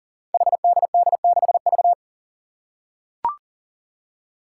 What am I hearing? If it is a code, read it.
Morse code, heard as a keyed tone, about 40 wpm, 700 Hz. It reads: HDD6V
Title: Five Challenging Letter-Number Combinations 40wpm